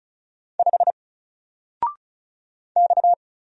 Morse code, heard as a keyed tone, about 36 wpm, 700 Hz. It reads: 5 X
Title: Single Character (Letter:Number:Punctuation:Prosign) Alt 35wpm